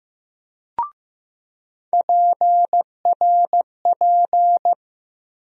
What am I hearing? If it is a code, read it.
Morse code, heard as a keyed tone, about 15 wpm, 700 Hz. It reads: PRP
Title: USA Call Sign Postfixes - 3 Letters 15wpm